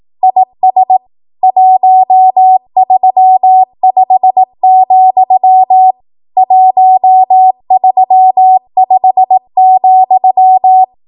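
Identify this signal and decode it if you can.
Morse code, heard as a keyed tone, about 18 wpm, 750 Hz. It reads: IS 135, 135,
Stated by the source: ARRL 18 wpm Morse code practice archive transmission.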